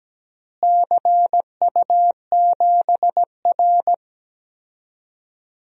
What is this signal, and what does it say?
Morse code, heard as a keyed tone, about 17 wpm, 700 Hz. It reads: CU7R
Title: DXCC-100 Call Signs - 4 Characters - Repeated 2x - 17wpm